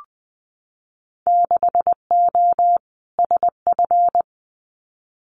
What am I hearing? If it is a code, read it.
Morse code, heard as a keyed tone, about 20 wpm, 700 Hz. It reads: 6O SF
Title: Winter Field Day Exchanges Encore 20wpm